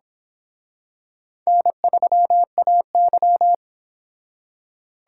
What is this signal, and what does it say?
Morse code, heard as a keyed tone, about 26 wpm, 700 Hz. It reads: N3AY
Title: Speed Racing - USA 1x2 Call Signs 22wpm